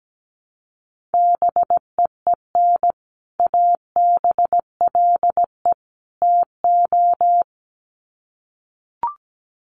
Morse code, heard as a keyed tone, about 17 wpm, 700 Hz. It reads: BEEN ABLE TO
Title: Sets of 3 Words 17wpm